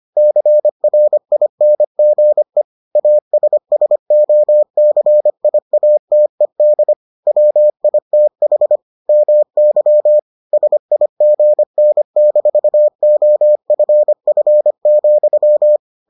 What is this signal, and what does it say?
Morse code, heard as a keyed tone, about 25 wpm, 600 Hz. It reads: CRINGE ASSOCIATED WITH MY SIGN-OFF,